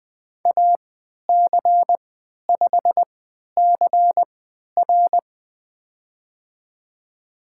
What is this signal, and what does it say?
Morse code, heard as a keyed tone, about 20 wpm, spaced as for 11 wpm, 700 Hz. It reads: AC5CR
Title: USA Call Signs - 5 Characters - 3x Character Spacing - 20wpm